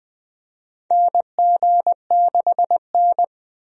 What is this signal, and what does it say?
Morse code, heard as a keyed tone, about 20 wpm, 700 Hz. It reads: NG6N
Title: USA Call Signs - 4 Characters - 20wpm